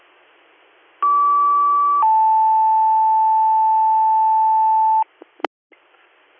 Two-tone page, heard as a one-second tone, then a three-second tone, about 1150 then 880 Hz. It is heard from 1.0 to 5.0 s.